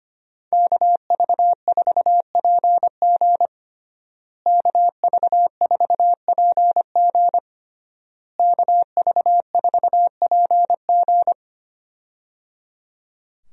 Morse code, heard as a keyed tone, about 25 wpm, 700 Hz. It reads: KV4PG KV4PG KV4PG
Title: USA Call Signs - 5 Characters - Repeated 3x - 25wpm